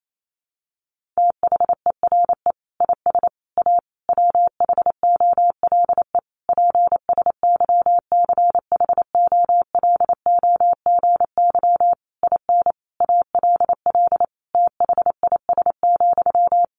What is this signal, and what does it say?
Morse code, heard as a keyed tone, about 28 wpm, 700 Hz. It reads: THERE IS A WHOLE PSYCHOLOGY IN ALL THIS,
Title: Sentences from Top 1400 Words 28wpm